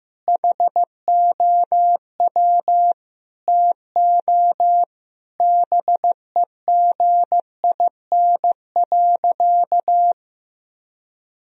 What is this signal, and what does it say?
Morse code, heard as a keyed tone, about 15 wpm, 700 Hz. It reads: HOW TO BEGIN.
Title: Sentences from Top 1000 Words - Review Part 1 - 15wpm